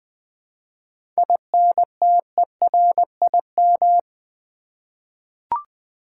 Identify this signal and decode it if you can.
Morse code, heard as a keyed tone, about 20 wpm, 700 Hz. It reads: INTERIM
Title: 7 Letter Words - First 3 Letters Given - 20wpm